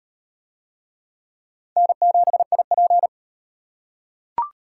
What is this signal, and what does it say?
Morse code, heard as a keyed tone, about 37 wpm, 700 Hz. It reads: N7IP